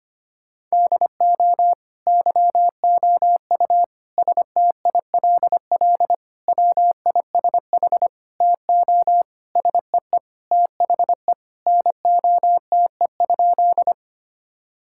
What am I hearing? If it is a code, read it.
Morse code, heard as a keyed tone, about 25 wpm, 700 Hz. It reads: DO YOU STILL WISH TO SEE THE NOTE?
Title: Sentences from Top 1600 Words 25wpm